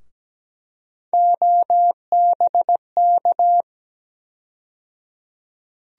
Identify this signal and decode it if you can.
Morse code, heard as a keyed tone, about 17 wpm, 700 Hz. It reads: OBK
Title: USA Call Sign Postfixes - 3 Letters 17wpm